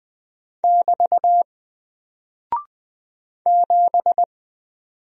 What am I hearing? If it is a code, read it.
Morse code, heard as a keyed tone, about 20 wpm, 700 Hz. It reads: <BT> 7
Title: Single Character (Letter:Number:Punctuation:Prosign) Alt 20wpm